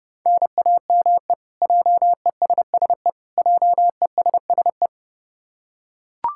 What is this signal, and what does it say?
Morse code, heard as a keyed tone, about 30 wpm, 700 Hz. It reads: NAME JESSE JESSE
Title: K1USN SST QSO Practice 30wpm